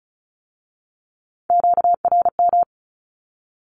Morse code, heard as a keyed tone, about 36 wpm, 700 Hz. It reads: QRM
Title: Q-Codes with Spoken Meaning - 35wpm